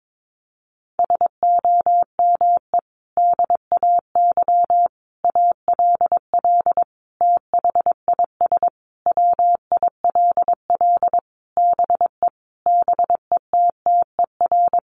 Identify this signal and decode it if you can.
Morse code, heard as a keyed tone, about 22 wpm, 700 Hz. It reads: SOME DAY ALL THIS WILL BE BETTER
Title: Sentences from Top 1300 Words 22wpm